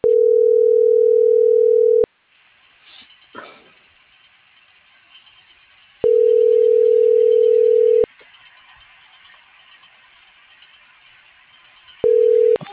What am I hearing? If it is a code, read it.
Ringback tone (North America), 440 + 480 Hz.
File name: CENTRAL NAVIERAS / out alejandro.cruz